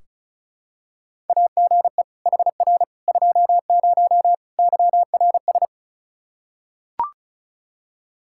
Morse code, heard as a keyed tone, about 36 wpm, 700 Hz. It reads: AGE HR 20 YRS
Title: Practice QSO - Exchange 3 - More Info - 35wpm